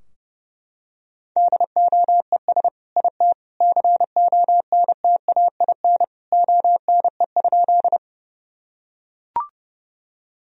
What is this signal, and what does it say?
Morse code, heard as a keyed tone, about 30 wpm, 700 Hz. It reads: DOES IT CONTAIN ONE?
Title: Sentences from Top 1000 Words - Review Part 2 - 30wpm